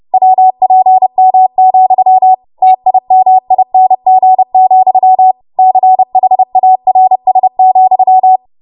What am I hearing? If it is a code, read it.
Morse code, heard as a keyed tone, about 30 wpm, 750 Hz. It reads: WPM, TIMING, CHARS,
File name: Morse / 30WPM